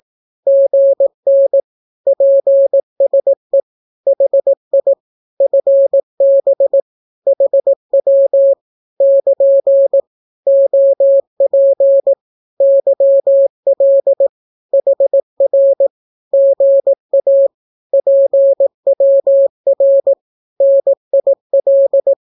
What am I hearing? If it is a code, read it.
Morse code, heard as a keyed tone, about 18 wpm, 550 Hz. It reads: GN PSE HI FB HW <KN> OP YL HR GA PWR NIL
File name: naukapodstaw / skroty / skroty